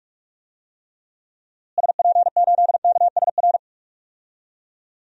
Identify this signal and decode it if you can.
Morse code, heard as a keyed tone, about 41 wpm, 700 Hz. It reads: SP8KSR
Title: DXCC-100 Call Signs - 6 Characters - Repeated 3x - 45wpm